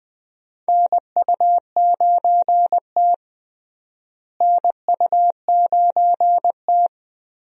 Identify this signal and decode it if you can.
Morse code, heard as a keyed tone, about 20 wpm, 700 Hz. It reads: NU9T NU9T
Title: USA Call Signs - 4 Characters - Repeated 2x - 20wpm